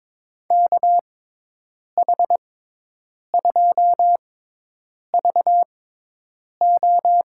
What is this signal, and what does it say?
Morse code, heard as a keyed tone, about 22 wpm, spaced as for 8 wpm, 700 Hz. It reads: KH2VO
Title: USA Call Signs - 5 Characters - 6x Character Spacing - 22wpm